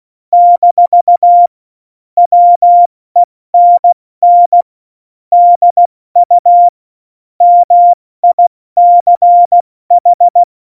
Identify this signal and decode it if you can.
Morse code, heard as a keyed tone, about 16 wpm, 700 Hz. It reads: - WENN DU MICH
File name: Morsekurs / Witze / Witz 18. September 2018, 70 F80